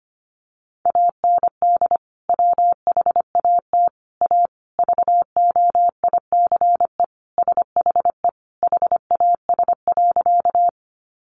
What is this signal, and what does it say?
Morse code, heard as a keyed tone, about 25 wpm, 700 Hz. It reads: AND WHAT A VOICE SHE HAS.
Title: Sentences from Top 500 Words 25wpm